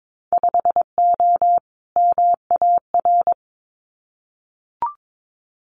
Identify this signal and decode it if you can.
Morse code, heard as a keyed tone, about 22 wpm, 700 Hz. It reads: 5O MAR
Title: Winter Field Day Exchanges Encore 22wpm